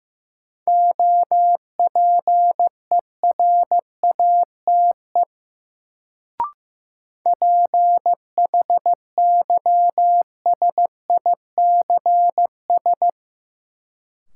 Morse code, heard as a keyed tone, about 15 wpm, 700 Hz. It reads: OPERATE PHYSICS
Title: Sentences from Top 1500 Words - New Word Drill Alt - 15wpm